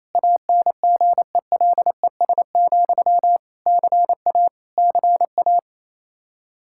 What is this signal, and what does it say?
Morse code, heard as a keyed tone, about 28 wpm, 700 Hz. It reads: ANGELES, CA CA
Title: Practice QSO - Exchange 1 - Beginning - 28wpm